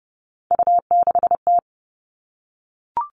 Morse code, heard as a keyed tone, about 30 wpm, 700 Hz. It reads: U6T
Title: All Cut Numbers - 3-Digit 30wpm